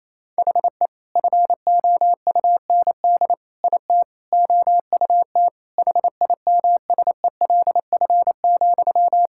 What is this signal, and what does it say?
Morse code, heard as a keyed tone, about 28 wpm, 700 Hz. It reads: HE FOUND IT OUT HIMSELF,
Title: Sentences from Top 1600 Words 28wpm